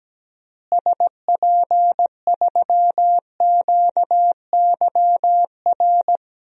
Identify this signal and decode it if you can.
Morse code, heard as a keyed tone, about 17 wpm, 700 Hz. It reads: SP3QYR
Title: DXCC-100 Call Signs - 6 Characters - Repeated 3x - 17wpm